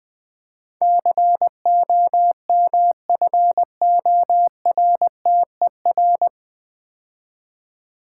Morse code, heard as a keyed tone, about 20 wpm, 700 Hz. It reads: COMFORTER
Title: Easy Words - Speed-Racing with Hints - 20wpm